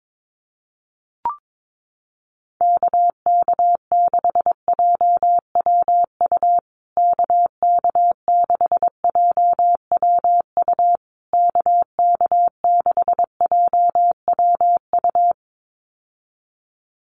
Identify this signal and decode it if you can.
Morse code, heard as a keyed tone, about 22 wpm, 700 Hz. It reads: KK6JWU KK6JWU KK6JWU
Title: USA Call Signs Repeated 3x 22wpm